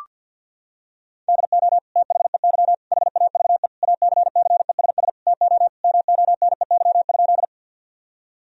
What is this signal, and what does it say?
Morse code, heard as a keyed tone, about 46 wpm, 700 Hz. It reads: DO THEY HAVE ACCESS TO MONEY?
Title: Sentences from Top 1000 Words - Review Part 4 - 50wpm